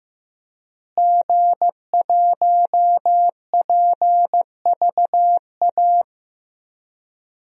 Morse code, heard as a keyed tone, about 16 wpm, 700 Hz. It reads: G1PVA
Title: DXCC-100 Call Signs - 5 Characters - Repeated 2x - 15wpm